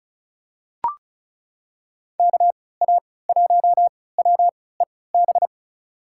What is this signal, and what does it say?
Morse code, heard as a keyed tone, about 36 wpm, spaced as for 20 wpm, 700 Hz. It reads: KA1WEB
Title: USA Call Signs - 3x Character Spacing - 35wpm